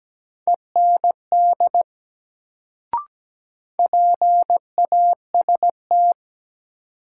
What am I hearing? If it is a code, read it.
Morse code, heard as a keyed tone, about 17 wpm, 700 Hz. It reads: END PAST
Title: Sets of 2 Words - New Word Drill - 17wpm